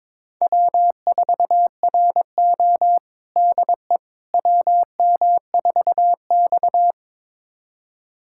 Morse code, heard as a keyed tone, about 22 wpm, 700 Hz. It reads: W4RO DE WM4X
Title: USA Callsign DE Callsign - 4 Characters - 22wpm